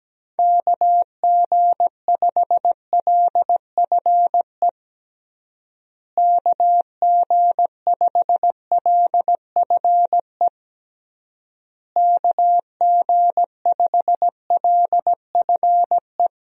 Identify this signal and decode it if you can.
Morse code, heard as a keyed tone, about 17 wpm, 700 Hz. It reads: KG5LFE KG5LFE KG5LFE
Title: USA Call Signs - 6 Characters - Repeated 3x - 17wpm